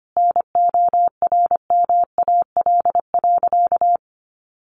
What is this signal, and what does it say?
Morse code, heard as a keyed tone, about 25 wpm, 700 Hz. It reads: NORMAL.
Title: Sentences from Top 1200 Words 25wpm